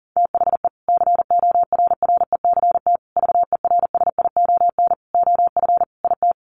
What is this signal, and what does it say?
Morse code, heard as a keyed tone, about 37 wpm, 700 Hz. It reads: THE CORRECT VERSION OF IT
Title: Sentences from Top 1400 Words 40wpm